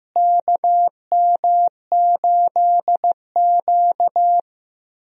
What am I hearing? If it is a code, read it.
Morse code, heard as a keyed tone, about 15 wpm, 700 Hz. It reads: KM8Q
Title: USA Call Signs - 4 Characters - Repeated 2x - 15wpm